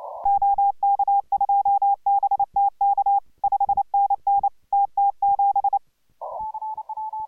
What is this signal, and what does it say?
Morse code, heard as a keyed tone, about 28 wpm, 810 Hz. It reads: OK2BTK 5NN TT7